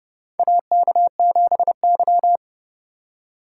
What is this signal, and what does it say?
Morse code, heard as a keyed tone, about 30 wpm, 700 Hz. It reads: AK7Y
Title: USA Call Signs - 4 Characters - 3x Character Spacing - 30wpm